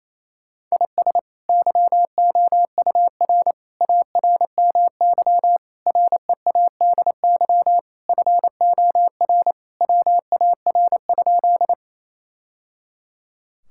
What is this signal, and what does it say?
Morse code, heard as a keyed tone, about 28 wpm, 700 Hz. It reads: IS YOUR ARMY READY FOR WAR?
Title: Sentences from Top 1000 Words - Review Part 5 - 28wpm